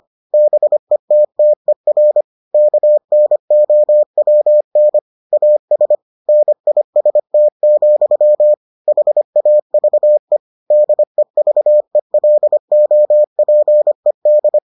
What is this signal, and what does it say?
Morse code, heard as a keyed tone, about 25 wpm, 600 Hz. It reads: BETTER KNOWN AS NIST, HAVE DEVELOPED